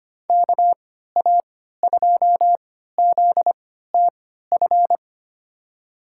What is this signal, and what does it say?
Morse code, heard as a keyed tone, about 25 wpm, spaced as for 14 wpm, 700 Hz. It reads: KA2ZTF
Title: USA Call Signs - 6 Characters - 3x Character Spacing - 25wpm